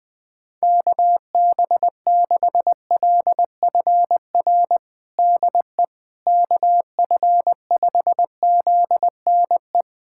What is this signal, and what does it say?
Morse code, heard as a keyed tone, about 21 wpm, 700 Hz. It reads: KB6LFR DE KF5ZNE
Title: USA Callsign DE Callsign 20wpm